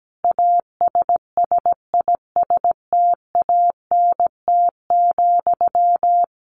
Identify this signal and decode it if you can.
Morse code, heard as a keyed tone, about 17 wpm, 700 Hz. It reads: ASSISTANT,
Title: A Sound Of Thunder 17wpm